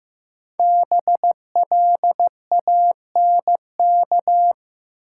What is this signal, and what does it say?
Morse code, heard as a keyed tone, about 15 wpm, 700 Hz. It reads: BLANK
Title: Sentences from Top 1500 Words - New Word Drill Alt - 15wpm